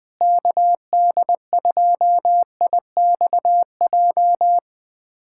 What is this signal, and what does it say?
Morse code, heard as a keyed tone, about 21 wpm, 700 Hz. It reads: KD2IXJ
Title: USA Callsign DE Callsign - 6 Characters - 20wpm